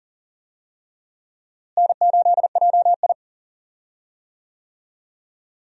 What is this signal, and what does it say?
Morse code, heard as a keyed tone, about 37 wpm, 700 Hz. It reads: N8JI
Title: USA Call Signs - 4 Characters - Repeated 3x - 40wpm